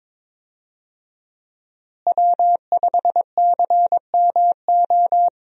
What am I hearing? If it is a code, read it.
Morse code, heard as a keyed tone, about 22 wpm, 700 Hz. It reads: W5CMO
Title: USA Call Signs - 5 Characters - Repeated 2x - 22wpm